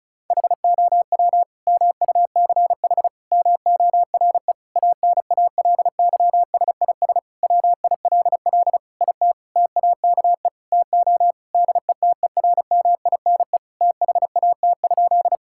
Morse code, heard as a keyed tone, about 36 wpm, 700 Hz. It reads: HOW MUCH MORE ANALYSIS WILL IT TAKE TO DETERMINE THAT?
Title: Sentences from Top 1000 Words - Review Part 5 - 35wpm